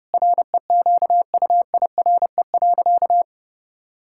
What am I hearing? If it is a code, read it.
Morse code, heard as a keyed tone, about 30 wpm, 700 Hz. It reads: REQUIRE.
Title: Sentences from Top 1000 Words - Review Part 6 - 30wpm